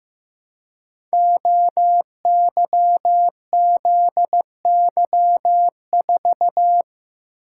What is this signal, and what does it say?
Morse code, heard as a keyed tone, about 15 wpm, 700 Hz. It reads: OYZY4
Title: Code Groups - 5 Letter:Numbers - 15wpm